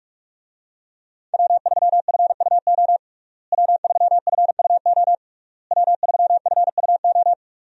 Morse code, heard as a keyed tone, about 45 wpm, 700 Hz. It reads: W3FUO W3FUO W3FUO
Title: USA Call Signs - 5 Characters - Repeated 3x - 45wpm